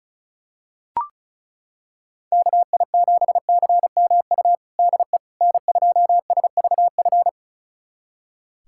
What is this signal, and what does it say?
Morse code, heard as a keyed tone, about 34 wpm, 690 Hz. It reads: KI7CMU DE N2SVF
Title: USA Callsign DE Callsign 35wpm